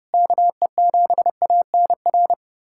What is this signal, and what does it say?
Morse code, heard as a keyed tone, about 32 wpm, 700 Hz. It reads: KE7ANR
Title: USA Callsign DE Callsign - 6 Characters - 30wpm